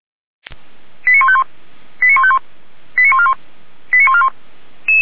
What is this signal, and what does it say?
Five-tone selective call (ZVEI), heard as quick runs of five tones, tones of about 70 ms. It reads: 89151 89151 89141 89141